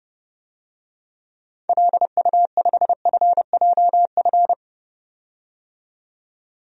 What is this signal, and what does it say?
Morse code, heard as a keyed tone, about 32 wpm, 700 Hz. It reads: LU5FJF
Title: DXCC-100 Call Signs - 6 Characters - 30wpm